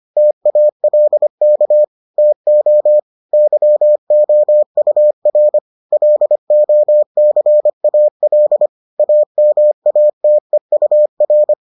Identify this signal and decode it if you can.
Morse code, heard as a keyed tone, about 25 wpm, 600 Hz. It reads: TALK TO YOUR LOCAL AMATEUR